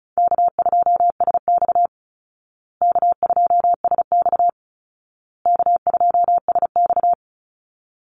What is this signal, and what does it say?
Morse code, heard as a keyed tone, about 34 wpm, 700 Hz. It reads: K2SX K2SX K2SX
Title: Call Signs Repeated 2x - 35wpm